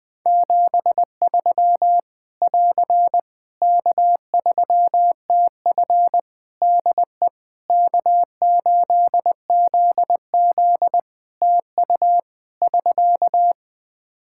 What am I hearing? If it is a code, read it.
Morse code, heard as a keyed tone, about 20 wpm, 700 Hz. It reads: 73 <AR> K3TF DE K8ZZ TU <SK>
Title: K1USN SST QSO Practice 20wpm